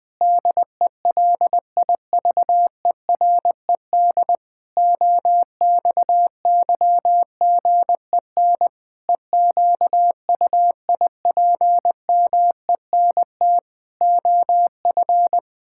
Morse code, heard as a keyed tone, about 19 wpm, 700 Hz. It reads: DELIVERED OXYGEN EQUIPMENT OF
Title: Pangram Challenge v2 - 20wpm